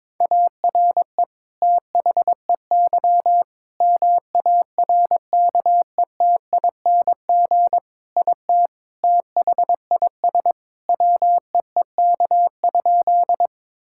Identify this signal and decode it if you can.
Morse code, heard as a keyed tone, about 21 wpm, 700 Hz. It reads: ARE THEY MARKETING IT THIS WEEK?
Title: Sentences from Top 1000 Words - Review Part 5 - 22wpm